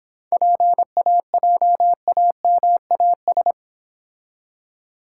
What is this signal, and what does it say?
Morse code, heard as a keyed tone, about 26 wpm, 700 Hz. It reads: PAJAMAS